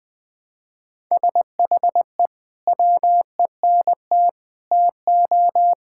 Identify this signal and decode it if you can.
Morse code, heard as a keyed tone, about 19 wpm, 700 Hz. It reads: SHE WENT TO
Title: Sentences with 2-Word Phrases - Spoken Beginning - 20wpm